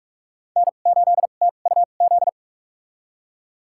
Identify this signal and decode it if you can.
Morse code, heard as a keyed tone, about 47 wpm, spaced as for 33 wpm, 700 Hz. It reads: N8TUZ
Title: USA Call Signs - 5 Characters - 2x Character Spacing - 45wpm